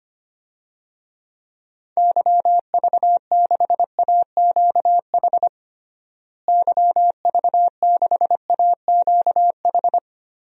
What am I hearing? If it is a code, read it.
Morse code, heard as a keyed tone, about 25 wpm, 700 Hz. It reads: YV6AQH YV6AQH
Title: DXCC-100 Call Signs - 6 Characters - Repeated 2x - 25wpm